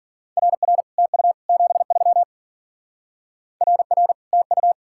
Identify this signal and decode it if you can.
Morse code, heard as a keyed tone, about 49 wpm, 700 Hz. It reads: RR TU 73 RR TU
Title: Speed-Racing Common QSO Phrases Encore - 40wpm